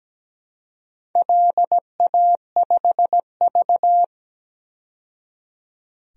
Morse code, heard as a keyed tone, about 17 wpm, 700 Hz. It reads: LA5V
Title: DXCC-100 Call Signs - 4 Characters - 17wpm